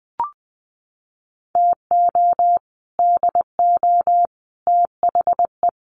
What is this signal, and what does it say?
Morse code, heard as a keyed tone, about 20 wpm, 700 Hz. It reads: TO DO THE
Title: Sets of 5 Words 20wpm